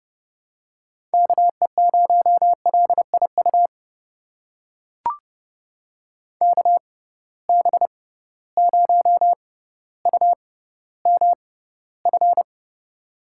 Morse code, heard as a keyed tone, about 30 wpm, spaced as for 15 wpm, 700 Hz. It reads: KE0LIU KB0UMF
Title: USA Call Signs - 6 Characters - 6x Character Spacing - 30wpm